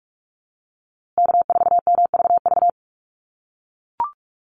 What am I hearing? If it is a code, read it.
Morse code, heard as a keyed tone, about 44 wpm, 700 Hz. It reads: K4MVV
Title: USA Call Signs Repeated 3x 45wpm